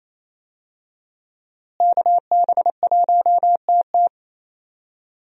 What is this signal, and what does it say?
Morse code, heard as a keyed tone, about 27 wpm, 700 Hz. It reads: KB1TT
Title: USA Call Signs - 5 Characters - Repeated 2x - 28wpm